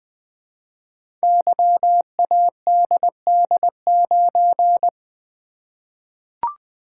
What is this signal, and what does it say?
Morse code, heard as a keyed tone, about 20 wpm, 700 Hz. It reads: YADD9
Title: Code Groups - 5 Letter:Numbers - 20wpm